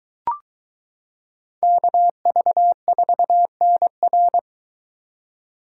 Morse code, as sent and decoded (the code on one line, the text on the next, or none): -.- ...- ....- -. .-.
KV4NR